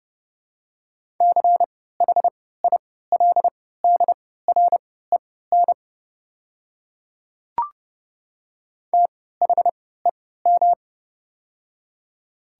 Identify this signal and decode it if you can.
Morse code, heard as a keyed tone, about 32 wpm, spaced as for 17 wpm, 700 Hz. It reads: CHILDREN THEM
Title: 3x Character Spacing - Top 300 Words - 30wpm